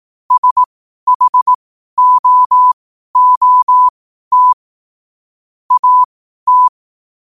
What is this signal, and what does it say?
Morse code, heard as a keyed tone, about 18 wpm, spaced as for 12 wpm, 1000 Hz. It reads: SHOOT AT
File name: qotdfiles / 12wpm